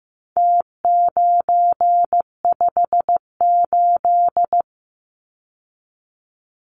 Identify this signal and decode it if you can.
Morse code, heard as a keyed tone, about 15 wpm, 700 Hz. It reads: T958
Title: Numbers - Contest Style - 4-Digit 15wpm